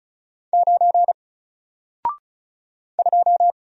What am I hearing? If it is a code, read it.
Morse code, heard as a keyed tone, about 36 wpm, 700 Hz. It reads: 9 2
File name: Single-Letter-Number-35wpm